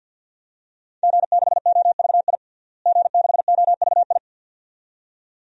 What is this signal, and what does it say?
Morse code, heard as a keyed tone, about 47 wpm, 700 Hz. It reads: G6OVI G6OVI
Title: DXCC-100 Call Signs - 5 Characters - Repeated 2x - 50wpm